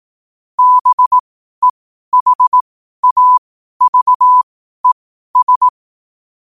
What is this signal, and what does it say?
Morse code, heard as a keyed tone, about 18 wpm, spaced as for 12 wpm, 1000 Hz. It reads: BEHAVES
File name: qotdfiles / 12wpm